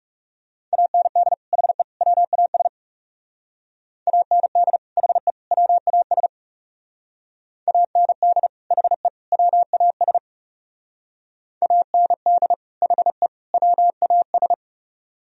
Morse code, heard as a keyed tone, about 37 wpm, 700 Hz. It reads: AND HE WAS AND HE WAS AND HE WAS AND HE WAS
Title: Speed-Racing - Sets of 3 Words 30wpm